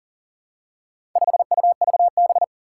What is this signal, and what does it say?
Morse code, heard as a keyed tone, about 38 wpm, 700 Hz. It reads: 5UVB
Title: Four Challenging Letter-Number Combinations 40wpm